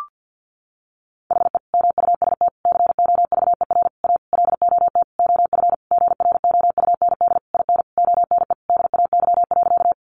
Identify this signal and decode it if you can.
Morse code, heard as a keyed tone, about 50 wpm, 700 Hz. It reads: HE MUST COVER A LOT OF GROUND IN ONE DAY.
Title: Sentences from Top 500 Words 50wpm